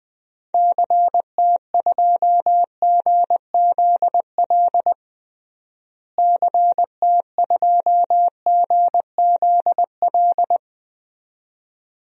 Morse code, heard as a keyed tone, about 20 wpm, 700 Hz. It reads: CT2GZL CT2GZL
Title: DXCC-100 Call Signs - 6 Characters - Repeated 2x - 20wpm